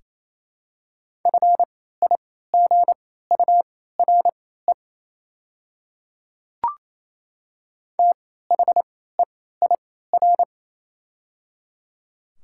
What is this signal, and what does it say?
Morse code, heard as a keyed tone, about 28 wpm, spaced as for 16 wpm, 690 Hz. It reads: FIGURE THEIR
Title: 3x Character Spacing - Top 500 Words - 28wpm